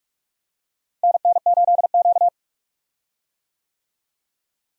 Morse code, heard as a keyed tone, about 46 wpm, 700 Hz. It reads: NN8Q
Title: USA Call Signs - 4 Characters - Repeated 2x - 45wpm